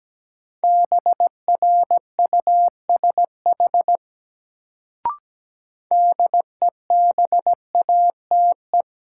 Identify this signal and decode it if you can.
Morse code, heard as a keyed tone, about 17 wpm, 700 Hz. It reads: BRUSH DEBATE
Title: Sentences from Top 1100 Words - New Word Drill Alt - 17wpm